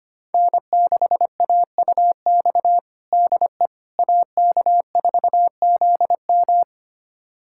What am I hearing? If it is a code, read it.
Morse code, heard as a keyed tone, about 25 wpm, 700 Hz. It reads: N6AUX DE AK4ZM
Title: USA Callsign DE Callsign - 5 Characters - 25wpm